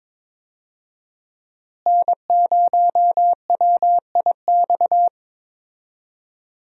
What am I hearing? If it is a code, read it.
Morse code, heard as a keyed tone, about 22 wpm, 700 Hz. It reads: N0WIX
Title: USA Call Signs - 5 Characters - Repeated 3x - 22wpm